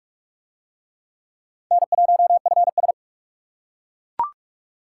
Morse code, heard as a keyed tone, about 38 wpm, 700 Hz. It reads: N1FS